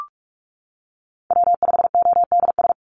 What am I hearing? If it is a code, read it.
Morse code, heard as a keyed tone, about 46 wpm, 700 Hz. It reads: W5ODS
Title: NA SOTA Chaser Callsigns 45wpm